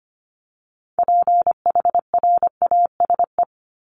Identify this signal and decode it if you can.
Morse code, heard as a keyed tone, about 25 wpm, 700 Hz. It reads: PHRASE
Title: Top 1501-1750 Words 25wpm